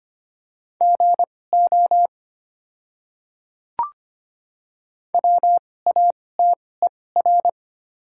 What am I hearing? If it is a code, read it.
Morse code, heard as a keyed tone, about 25 wpm, spaced as for 18 wpm, 690 Hz. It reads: GO WATER
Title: 2x Character Spacing - Top 100 Words - 25wpm